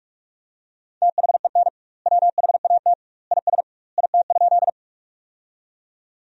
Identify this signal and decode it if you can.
Morse code, heard as a keyed tone, about 46 wpm, 700 Hz. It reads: THEN WHAT IS IT?
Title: 2x Word-Spacing - Sentences from Top 100 Words - 45wpm